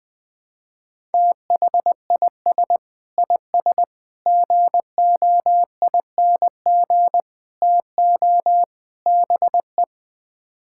Morse code, heard as a keyed tone, about 21 wpm, 700 Hz. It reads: THIS IS GOING TO BE
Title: Build A Phrase - 5-Word Phrases - 20wpm